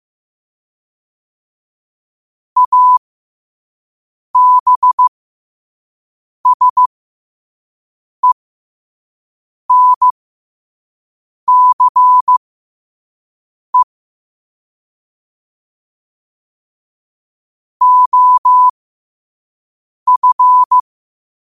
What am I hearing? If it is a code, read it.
Morse code, heard as a keyed tone, about 15 wpm, spaced as for 5 wpm, 1000 Hz. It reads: ABSENCE OF
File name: qotdfiles / 5wpm